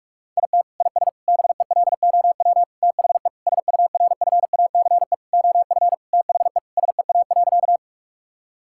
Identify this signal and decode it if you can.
Morse code, heard as a keyed tone, about 46 wpm, 700 Hz. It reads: IT IS BELOW THE SURFACE OF THE SEA.
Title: Sentences from Top 1000 Words - Review Part 5 - 45wpm